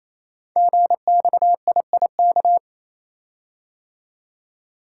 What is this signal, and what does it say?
Morse code, heard as a keyed tone, about 29 wpm, 700 Hz. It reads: GXIIK
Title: Code Groups - 5 Letters - 28wpm